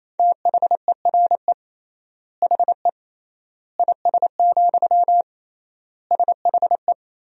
Morse code, heard as a keyed tone, about 28 wpm, 700 Hz. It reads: THERE HE IS, SHE
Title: 3x Word-Spacing - Sentences from Top 100 Words - 28wpm